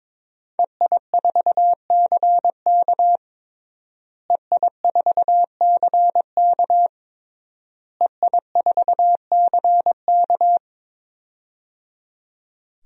Morse code, heard as a keyed tone, about 22 wpm, 700 Hz. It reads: EI4CK EI4CK EI4CK
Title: DXCC-100 Call Signs - 5 Characters - Repeated 3x - 22wpm